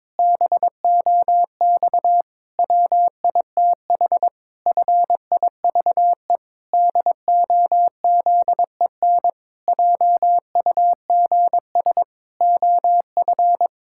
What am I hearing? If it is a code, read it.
Morse code, heard as a keyed tone, about 22 wpm, 700 Hz. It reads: BOX WITH FIVE DOZEN JUGS OF
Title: Pangram Challenge v2 - 22wpm